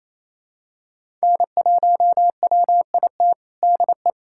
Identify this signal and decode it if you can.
Morse code, heard as a keyed tone, about 28 wpm, 700 Hz. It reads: N1WIT DE
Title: USA Callsign DE Callsign - 5 Characters - 28wpm